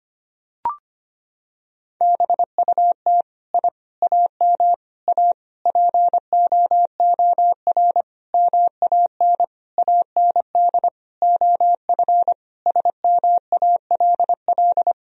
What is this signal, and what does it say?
Morse code, heard as a keyed tone, about 25 wpm, 700 Hz. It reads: BUT I AM A POOR MAN AND OF SMALL
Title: Sentences from Top 1000 Words - Review Part 3 - 25wpm